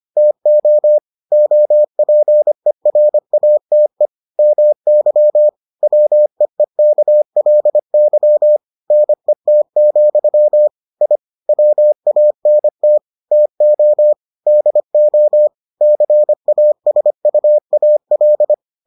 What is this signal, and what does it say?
Morse code, heard as a keyed tone, about 25 wpm, 600 Hz. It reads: TO OPERATE MY WEEKLY NET, I WANT TO DO CASUAL